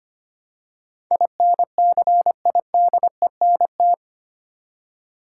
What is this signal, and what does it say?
Morse code, heard as a keyed tone, about 25 wpm, 700 Hz. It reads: INCIDENT
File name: 8 Letter Words - First 3 Letters Given - 25wpm